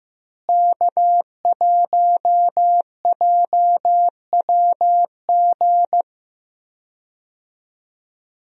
Morse code, heard as a keyed tone, about 16 wpm, 700 Hz. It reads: K1JWG
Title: USA Call Signs - 5 Characters - Repeated 3x - 15wpm